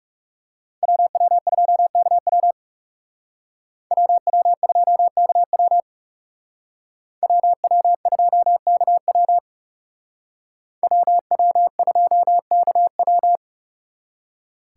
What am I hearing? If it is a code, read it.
Morse code, heard as a keyed tone, about 36 wpm, 700 Hz. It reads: WW2KW WW2KW WW2KW WW2KW
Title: Speed Racing - USA 2x2 Call Signs 30wpm